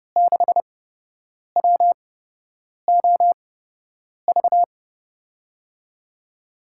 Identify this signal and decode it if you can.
Morse code, heard as a keyed tone, about 30 wpm, spaced as for 8 wpm, 700 Hz. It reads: 6WOV
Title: USA Call Signs - 5 Characters - 8x Character Spacing - 30wpm